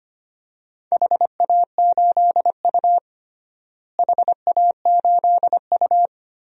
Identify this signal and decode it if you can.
Morse code, heard as a keyed tone, about 25 wpm, 700 Hz. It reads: HA8U HA8U
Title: DXCC-100 Call Signs - 4 Characters - Repeated 3x - 25wpm